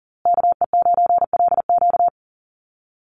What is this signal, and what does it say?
Morse code, heard as a keyed tone, about 40 wpm, 700 Hz. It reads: KE9LQ
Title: USA Call Signs Repeated 3x 40wpm